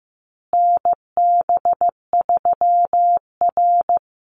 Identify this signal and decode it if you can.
Morse code, heard as a keyed tone, about 15 wpm, 700 Hz. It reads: NB3R
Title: Call Signs Repeated 2x - 15wpm